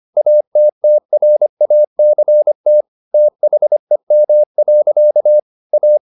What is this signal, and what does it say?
Morse code, heard as a keyed tone, about 25 wpm, 600 Hz. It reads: ATTRACT THEM. A
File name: the old man and the sea part 1 25wpm